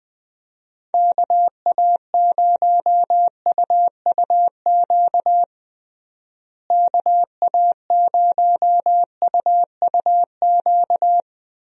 Morse code, heard as a keyed tone, about 20 wpm, 700 Hz. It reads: KA0UUQ KA0UUQ
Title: USA Call Signs - 6 Characters - Repeated 3x - 20wpm